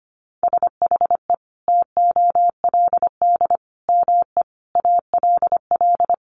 Morse code, heard as a keyed tone, about 25 wpm, 700 Hz. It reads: SHE TOLD ME ALL
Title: Sentences from Top 400 Words 25wpm